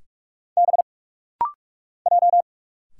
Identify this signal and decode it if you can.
Morse code, heard as a keyed tone, about 46 wpm, 700 Hz. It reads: B J
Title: Single Letters - Rapid-Fire 45wpm